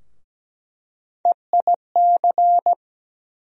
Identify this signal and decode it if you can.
Morse code, heard as a keyed tone, about 17 wpm, 700 Hz. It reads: EIC
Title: USA Call Sign Postfixes - 3 Letters 17wpm